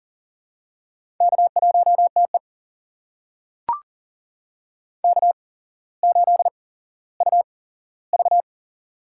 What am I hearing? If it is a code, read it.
Morse code, heard as a keyed tone, about 40 wpm, spaced as for 17 wpm, 700 Hz. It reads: K1TE K8UV
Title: USA Call Signs - 4 Characters - 8x Character Spacing - 40wpm